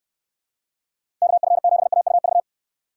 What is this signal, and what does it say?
Morse code, heard as a keyed tone, about 64 wpm, 700 Hz. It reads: KF7AUV